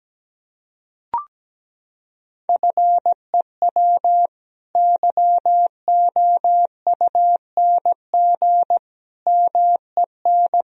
Morse code, heard as a keyed tone, about 17 wpm, 700 Hz. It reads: FEW YOUNG MEN
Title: Sentences from Top 1000 Words - Review Part 1 - 17wpm